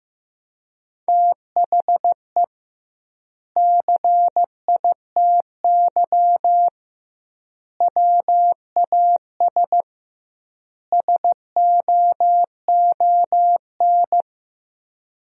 Morse code, heard as a keyed tone, about 15 wpm, 700 Hz. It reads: THE CITY WAS SOON
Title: 2x Word-Spacing - Sentences from Top 500 Words - 15wpm